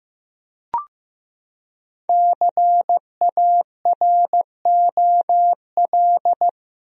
Morse code, heard as a keyed tone, about 15 wpm, 700 Hz. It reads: CAROL
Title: USA Common Names - Max 5 Letters - 15wpm